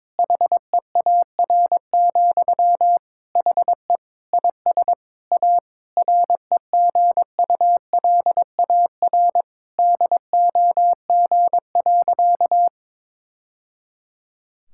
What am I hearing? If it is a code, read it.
Morse code, heard as a keyed tone, about 22 wpm, 700 Hz. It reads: HEAR, HE IS A REGULAR DOG.
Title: Sentences from Top 1000 Words - Review Part 6 - 22wpm